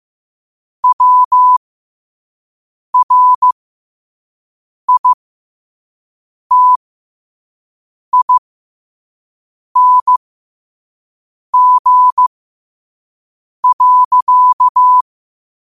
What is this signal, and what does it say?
Morse code, heard as a keyed tone, about 15 wpm, spaced as for 5 wpm, 1000 Hz. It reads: WRITING.
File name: qotdfiles / 5wpm